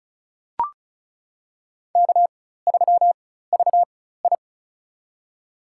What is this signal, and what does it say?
Morse code, heard as a keyed tone, about 36 wpm, spaced as for 16 wpm, 700 Hz. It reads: K3VI
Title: USA Call Signs - 4x Character Spacing - 35wpm